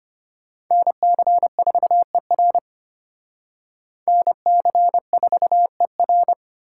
Morse code, heard as a keyed tone, about 33 wpm, 700 Hz. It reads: NC4ER NC4ER